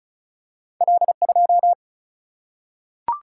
Morse code, heard as a keyed tone, about 35 wpm, 700 Hz. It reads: L2
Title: Single-Letter Single-Number 35wpm